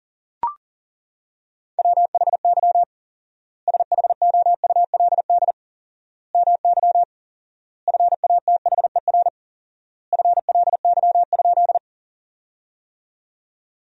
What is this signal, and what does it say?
Morse code, heard as a keyed tone, about 40 wpm, 700 Hz. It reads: WHY SHOULD MY FATHER FLY?
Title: 4x Word-Spacing - Sentences from Top 500 Words - 40wpm